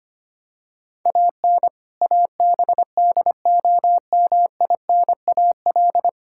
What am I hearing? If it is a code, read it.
Morse code, heard as a keyed tone, about 25 wpm, 700 Hz. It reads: AN ABDOMINAL
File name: practice / 25wpm / All-Age Appropriate Jokes - 25wpm